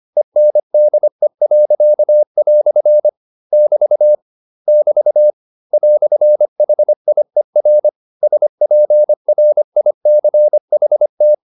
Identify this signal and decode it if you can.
Morse code, heard as a keyed tone, about 25 wpm, 600 Hz. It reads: ENDE." <BT> <BT> "HIER SPRICHT